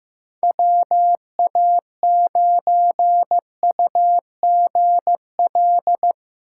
Morse code, heard as a keyed tone, about 15 wpm, 700 Hz. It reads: WA9UGL
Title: USA Call Signs - 8x Character Spacing - 15wpm